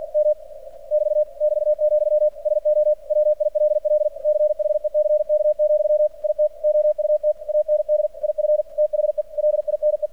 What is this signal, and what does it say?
Morse code, heard as a keyed tone, about 50 wpm, 600 Hz, with radio noise. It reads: M <BT> <BT>, SO WILL CHECK, IT OUT AND IF THE LINE
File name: archived / CRSnet